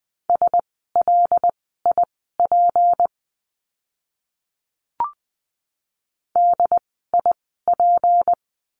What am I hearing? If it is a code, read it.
Morse code, heard as a keyed tone, about 20 wpm, spaced as for 14 wpm, 700 Hz. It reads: SLIP DIP
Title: 153 Lesson 25f - Letter p - Words 20wpm